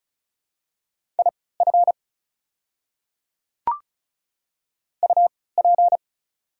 Morse code, heard as a keyed tone, about 34 wpm, spaced as for 20 wpm, 690 Hz. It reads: IF UP
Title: 3x Character Spacing - Top 300 Words - 35wpm